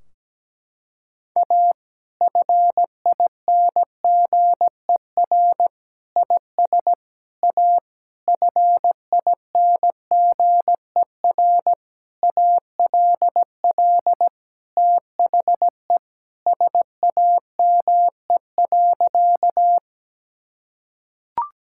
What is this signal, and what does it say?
Morse code, heard as a keyed tone, about 17 wpm, 700 Hz. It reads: A FINGER IS A FINGER ALL THE SAME.
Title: Sentences from Top 1600 Words 17wpm